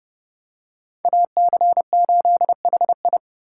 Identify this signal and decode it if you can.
Morse code, heard as a keyed tone, about 32 wpm, 700 Hz. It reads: AC8HI
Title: USA Call Signs - 5 Characters - 4x Character Spacing - 30wpm